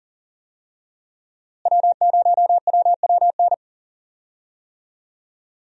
Morse code, heard as a keyed tone, about 40 wpm, 700 Hz. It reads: W0WWN
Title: USA Call Signs - 5 Characters - Repeated 3x - 40wpm